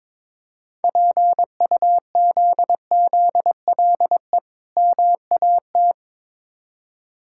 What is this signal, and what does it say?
Morse code, heard as a keyed tone, about 23 wpm, 690 Hz. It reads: PUZZLE MAT
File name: Difficult Words with Hints - 22wpm